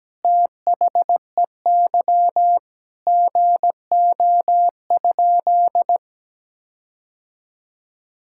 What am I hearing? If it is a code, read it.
Morse code, heard as a keyed tone, about 17 wpm, 700 Hz. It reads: THEY GO?
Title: Sentences from Top 1600 Words 17wpm